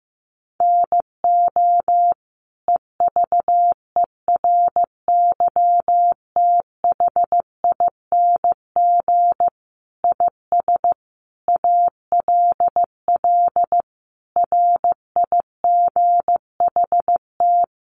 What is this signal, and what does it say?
Morse code, heard as a keyed tone, about 15 wpm, 700 Hz. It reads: NO EVERYTHING IS ALL RIGHT
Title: Sentences from Top 700 Words 15wpm